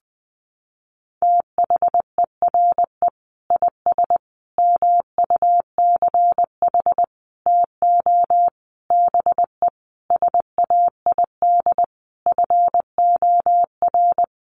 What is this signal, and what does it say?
Morse code, heard as a keyed tone, about 20 wpm, 700 Hz. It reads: THERE IS MUCH TO BE SAID FOR
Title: Sentences from Top 1300 Words 20wpm